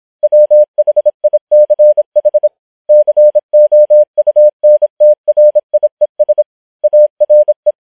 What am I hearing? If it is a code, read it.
Morse code, heard as a keyed tone, about 26 wpm, 600 Hz. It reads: WHICH COUNTRIES ARE